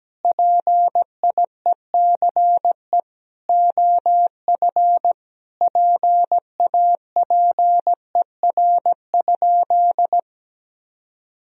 Morse code, heard as a keyed tone, about 17 wpm, 700 Hz. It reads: PIECE OF PAPER?
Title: Sentences from Top 1000 Words - Review Part 2 - 17wpm